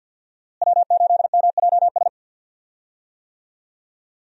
Morse code, heard as a keyed tone, about 50 wpm, 700 Hz. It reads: W8MJS
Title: USA Call Signs - 5 Characters - Repeated 2x - 50wpm